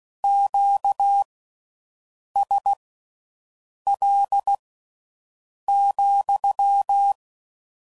Morse code, heard as a keyed tone, about 16 wpm, spaced as for 6 wpm, 800 Hz. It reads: QSL,